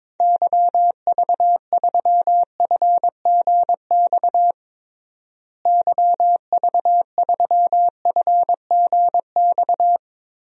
Morse code, heard as a keyed tone, about 22 wpm, 700 Hz. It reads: YV3FGX YV3FGX
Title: DXCC-100 Call Signs - 6 Characters - Repeated 2x - 22wpm